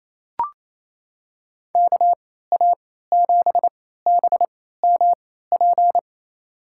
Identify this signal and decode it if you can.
Morse code, heard as a keyed tone, about 28 wpm, spaced as for 16 wpm, 700 Hz. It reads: KA7BMP
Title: USA Call Signs - 6 Characters - 3x Character Spacing - 28wpm